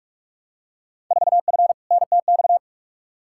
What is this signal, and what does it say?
Morse code, heard as a keyed tone, about 46 wpm, 700 Hz. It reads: 4F NTX
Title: Field Day Exchanges Encore - 45wpm